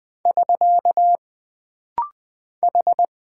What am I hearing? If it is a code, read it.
Morse code, heard as a keyed tone, about 20 wpm, 700 Hz. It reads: <SK> H
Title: Single Character (Letter:Number:Punctuation:Prosign) - Rapid-Fire - 20wpm